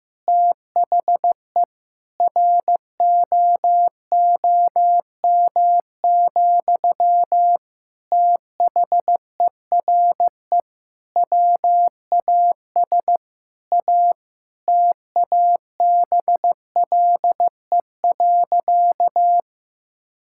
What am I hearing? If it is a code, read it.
Morse code, heard as a keyed tone, about 15 wpm, 700 Hz. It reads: THE ROOM, THERE WAS A TABLE.
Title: Sentences from Top 1000 Words - Review Part 6 - 15wpm